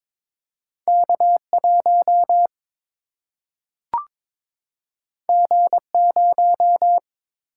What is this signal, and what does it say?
Morse code, heard as a keyed tone, about 22 wpm, 700 Hz. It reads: K 1 G 0
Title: Single-Letter Single-Number 22wpm